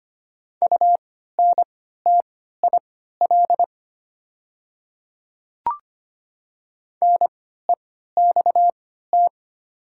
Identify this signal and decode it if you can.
Morse code, heard as a keyed tone, about 25 wpm, spaced as for 14 wpm, 700 Hz. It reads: UNTIL NEXT
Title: 3x Character Spacing - Top 300 Words - 25wpm